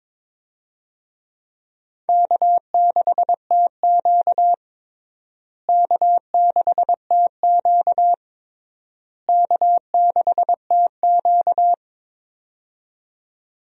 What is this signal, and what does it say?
Morse code, heard as a keyed tone, about 22 wpm, 700 Hz. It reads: K6TQ K6TQ K6TQ
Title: USA Call Signs - 4 Characters - Repeated 3x - 22wpm